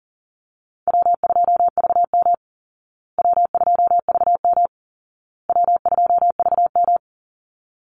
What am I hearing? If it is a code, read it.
Morse code, heard as a keyed tone, about 40 wpm, 700 Hz. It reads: W2VM W2VM W2VM
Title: Call Signs Repeated 2x - 40wpm